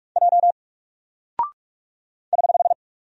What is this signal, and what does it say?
Morse code, heard as a keyed tone, about 44 wpm, 700 Hz. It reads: J <HH>
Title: Single Character (Letter:Number:Punctuation:Prosign) Alt 45wpm